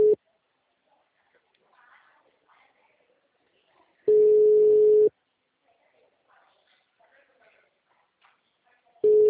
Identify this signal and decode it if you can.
Ringback tone (Europe, 425 Hz), with speech.
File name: out- -2002- - - .50836